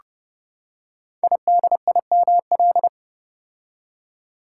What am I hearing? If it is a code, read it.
Morse code, heard as a keyed tone, about 30 wpm, 700 Hz. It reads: IDIML
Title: Code Groups - 5 Letter:Numbers - 30wpm